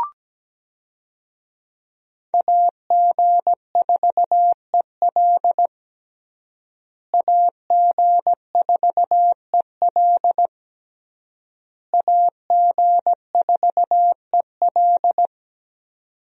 Morse code, heard as a keyed tone, about 17 wpm, 700 Hz. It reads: AG4EL AG4EL AG4EL
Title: USA Call Signs - 5 Characters - Repeated 3x - 17wpm